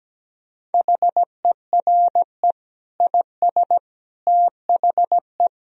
Morse code, heard as a keyed tone, about 17 wpm, 700 Hz. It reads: HERE IS THE
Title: Sentences from Top 1000 Words - Review Part 2 - 17wpm